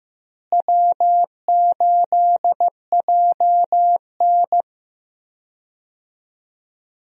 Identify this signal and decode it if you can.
Morse code, heard as a keyed tone, about 16 wpm, 700 Hz. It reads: W8JN
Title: USA Call Signs - 4 Characters - Repeated 2x - 15wpm